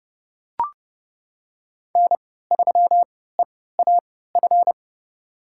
Morse code, heard as a keyed tone, about 30 wpm, spaced as for 17 wpm, 700 Hz. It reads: N3EAF
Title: USA Call Signs - 5 Characters - 3x Character Spacing - 30wpm